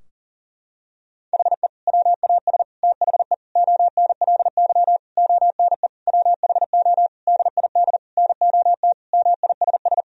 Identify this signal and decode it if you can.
Morse code, heard as a keyed tone, about 40 wpm, 700 Hz. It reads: HE WAS THE ONLY ONE WHO DID NOT MISS
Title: Sentences from Top 1000 Words - Review Part 3 - 40wpm